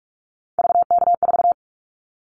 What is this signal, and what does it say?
Morse code, heard as a keyed tone, about 46 wpm, 700 Hz. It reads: VK4
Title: Callsign Prefixes for Top 100 DXCC - 45wpm